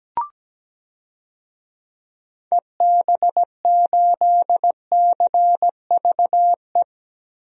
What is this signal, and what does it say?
Morse code, heard as a keyed tone, about 17 wpm, 700 Hz. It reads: EB8CVE
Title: DXCC-100 Call Signs - 6 Characters - Repeated 3x - 17wpm